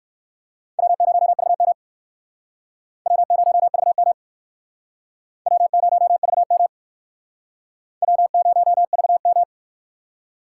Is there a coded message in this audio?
Yes: Morse code from 0.8 to 9.4 s, heard as a keyed tone, about 59 wpm, 700 Hz. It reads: W0VM W0VM W0VM W0VM